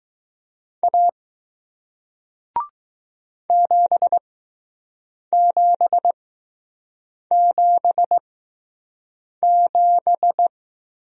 Morse code, heard as a keyed tone, about 20 wpm, spaced as for 6 wpm, 700 Hz. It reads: A 7777